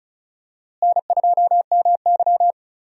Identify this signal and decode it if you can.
Morse code, heard as a keyed tone, about 34 wpm, 700 Hz. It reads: N2MY
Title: USA Call Signs - 4 Characters - Repeated 2x - 35wpm